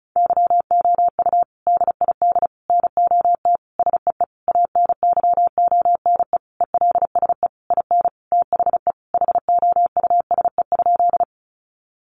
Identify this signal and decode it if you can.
Morse code, heard as a keyed tone, about 34 wpm, 700 Hz. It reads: YOU DID NOT SEE ANYONE ELSE IN THE HOUSE?
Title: Sentences from Top 900 Words 35wpm